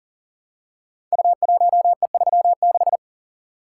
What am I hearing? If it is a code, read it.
Morse code, heard as a keyed tone, about 42 wpm, 700 Hz. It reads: U1E36
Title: Code Groups - 5 Letter:Numbers - 40wpm